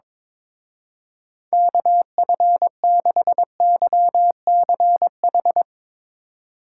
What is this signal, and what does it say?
Morse code, heard as a keyed tone, about 22 wpm, 700 Hz. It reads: KF6YCH
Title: Speed Racing - USA Call Signs 22wpm